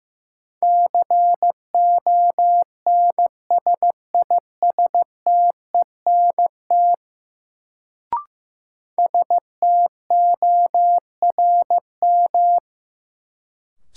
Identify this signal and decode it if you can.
Morse code, heard as a keyed tone, about 15 wpm, 700 Hz. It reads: CONSISTENT STORM
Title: Sentences from Top 1500 Words - New Word Drill - 15wpm